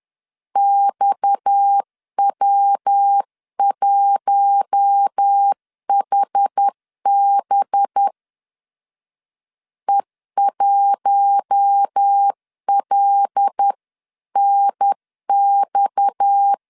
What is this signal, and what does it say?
Morse code, heard as a keyed tone, about 10 wpm, 800 Hz. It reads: XW1HB E1LNX
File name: CW / DET / 1 6 1500C S C 10wpm 4252